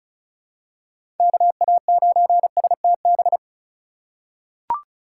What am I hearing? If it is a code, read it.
Morse code, heard as a keyed tone, about 34 wpm, 700 Hz. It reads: KA9STB
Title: USA Call Signs - 6 Characters - 3x Character Spacing - 35wpm